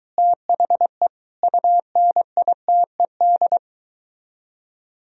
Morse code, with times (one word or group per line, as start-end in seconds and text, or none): 0.2-1.1 THE
1.4-3.6 UNITED